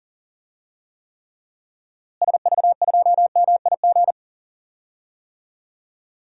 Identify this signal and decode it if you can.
Morse code, heard as a keyed tone, about 38 wpm, 700 Hz. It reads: SV2MIG
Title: DXCC-100 Call Signs - 6 Characters - Repeated 3x - 40wpm